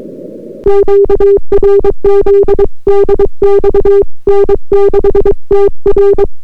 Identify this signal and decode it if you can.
Morse code, heard as a keyed tone, about 23 wpm, 400 Hz. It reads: QRZ DX N6TR